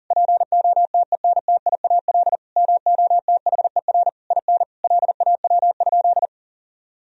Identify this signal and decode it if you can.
Morse code, heard as a keyed tone, about 39 wpm, 700 Hz. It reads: POTENTIAL MOTHER IN LAW?
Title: Sentences from Top 1000 Words - Review Part 4 - 40wpm